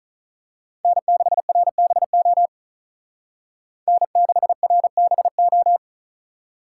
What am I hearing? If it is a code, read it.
Morse code, heard as a keyed tone, about 36 wpm, 700 Hz. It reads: N6RBO N6RBO
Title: Speed Racing - USA Call Signs 35wpm